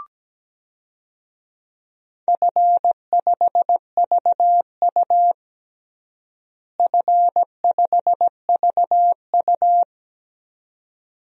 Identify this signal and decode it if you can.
Morse code, heard as a keyed tone, about 17 wpm, 700 Hz. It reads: F5VU F5VU
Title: DXCC-100 Call Signs - 4 Characters - Repeated 2x - 17wpm